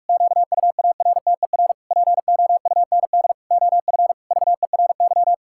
Morse code, heard as a keyed tone, about 46 wpm, 700 Hz. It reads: QUARTER POUND OF VERY
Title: Pangram Challenge v2 - 45wpm